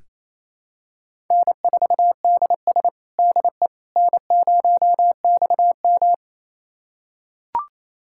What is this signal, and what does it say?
Morse code, heard as a keyed tone, about 27 wpm, 700 Hz. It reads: N4DS DE N0XM
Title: USA Callsign DE Callsign - 4 Characters - 28wpm